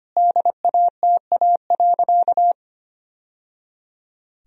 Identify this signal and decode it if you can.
Morse code, heard as a keyed tone, about 25 wpm, 700 Hz. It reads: DATA.
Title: Sentences from Top 1000 Words - Review Part 3 - 25wpm